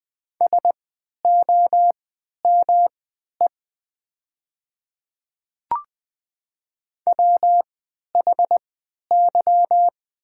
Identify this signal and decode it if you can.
Morse code, heard as a keyed tone, about 20 wpm, spaced as for 11 wpm, 700 Hz. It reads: SOME WHY
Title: 3x Character Spacing - Top 200 Words - 20wpm